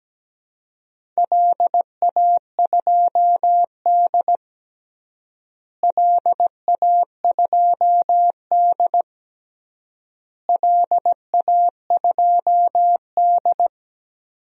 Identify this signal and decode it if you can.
Morse code, heard as a keyed tone, about 17 wpm, 700 Hz. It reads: LA2D LA2D LA2D
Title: DXCC-100 Call Signs - 4 Characters - Repeated 3x - 17wpm